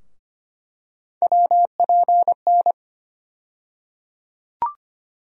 Morse code, heard as a keyed tone, about 25 wpm, 700 Hz. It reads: WPN
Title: USA Call Sign Postfixes - 3 Letters 25wpm